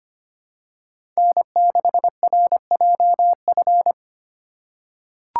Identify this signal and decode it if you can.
Morse code, heard as a keyed tone, about 25 wpm, 700 Hz. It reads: N6RJF
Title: USA Call Signs - 5 Characters - 4x Character Spacing - 25wpm